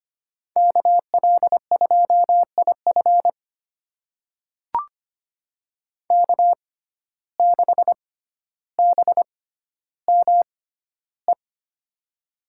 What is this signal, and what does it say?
Morse code, heard as a keyed tone, about 25 wpm, spaced as for 13 wpm, 700 Hz. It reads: KL2IF K6BME
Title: USA Call Signs - 5 Characters - 6x Character Spacing - 25wpm